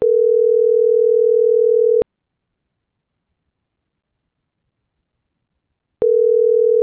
Ringback tone (North America), 440 + 480 Hz.
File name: CENTRAL MABE / out manuel.posadas